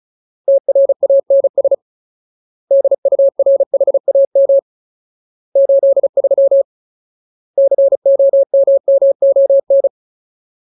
Morse code, heard as a keyed tone, about 34 wpm, 550 Hz. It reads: TRANS DURHAM 83 COMMON
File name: Ultimate Word List / 35WPM Ultimate